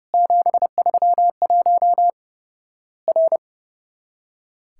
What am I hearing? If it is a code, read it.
Morse code, heard as a keyed tone, about 30 wpm, 700 Hz. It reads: 731 R
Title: Worked All Europe DX Contest - QTC Exchange - 30wpm